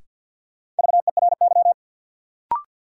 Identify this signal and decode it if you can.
Morse code, heard as a keyed tone, about 51 wpm, 700 Hz. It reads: VERY
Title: Sentences from Top 200 Words - New Word Drill - 50wpm